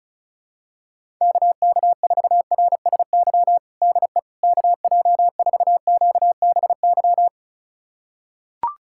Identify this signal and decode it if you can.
Morse code, heard as a keyed tone, about 34 wpm, 700 Hz. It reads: KK4RSY DE KJ4QBY
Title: USA Callsign DE Callsign - 6 Characters - 35wpm